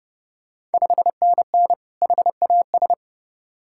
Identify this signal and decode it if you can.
Morse code, heard as a keyed tone, about 30 wpm, 700 Hz. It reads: 5NN HAS
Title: Ontario QSO Party Exchange 30wpm